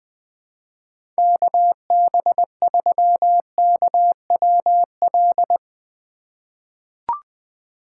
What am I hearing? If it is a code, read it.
Morse code, heard as a keyed tone, about 20 wpm, 700 Hz. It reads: KB3KWL
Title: USA Call Signs - 6 Characters - 4x Character Spacing - 20wpm